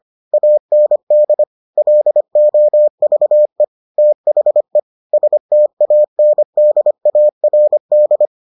Morse code, heard as a keyed tone, about 25 wpm, 600 Hz. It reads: AND LOVE THE STANDARD